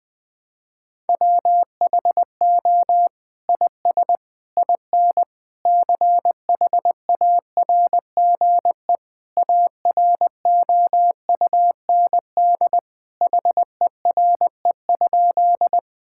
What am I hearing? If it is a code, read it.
Morse code, heard as a keyed tone, about 20 wpm, 700 Hz. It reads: WHO IS IN CHARGE AROUND HERE?
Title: Sentences from Top 1000 Words - Review Part 6 - 20wpm